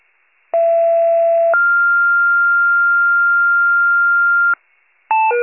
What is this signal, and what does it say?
Two-tone page, heard as a one-second tone, then a three-second tone, about 670 then 1400 Hz. It is heard from 0.5 to 4.5 s.